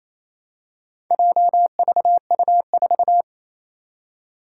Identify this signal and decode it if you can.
Morse code, heard as a keyed tone, about 27 wpm, 700 Hz. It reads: JVU4
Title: Four Challenging Letter-Number Combinations 28wpm